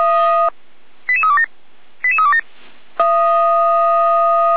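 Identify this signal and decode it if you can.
Five-tone selective call (ZVEI), heard as quick runs of five tones, tones of about 70 ms. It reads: 88327 88327
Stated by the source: ZVEI five-tone selective call received by SDR.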